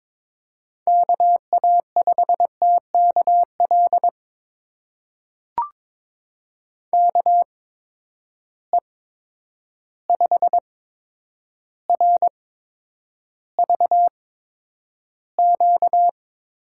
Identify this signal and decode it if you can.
Morse code, heard as a keyed tone, about 22 wpm, spaced as for 9 wpm, 700 Hz. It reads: KA5TKL KE5RVQ
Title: USA Call Signs - 6 Characters - 8x Character Spacing - 22wpm